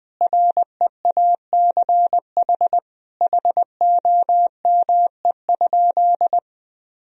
Morse code, heard as a keyed tone, about 20 wpm, 700 Hz. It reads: REACH HOME?
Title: Sentences from Top 1000 Words - Review Part 2 - 20wpm